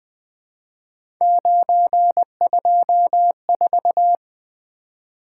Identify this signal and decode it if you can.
Morse code, heard as a keyed tone, about 20 wpm, 700 Hz. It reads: 924
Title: Three Letter-Number Combinations - 20wpm